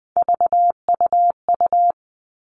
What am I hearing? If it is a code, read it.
Morse code, heard as a keyed tone, about 20 wpm, 700 Hz. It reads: VUU
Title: All Cut Numbers - 3-Digit 20wpm